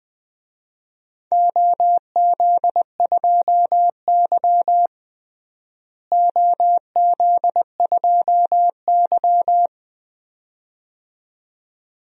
Morse code, heard as a keyed tone, about 21 wpm, 700 Hz. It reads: OZ2Y OZ2Y
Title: DXCC-100 Call Signs - 4 Characters - Repeated 2x - 20wpm